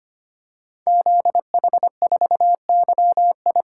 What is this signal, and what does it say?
Morse code, heard as a keyed tone, about 25 wpm, 700 Hz. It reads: ZH4YI
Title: Code Groups - 5 Letter:Numbers - 25wpm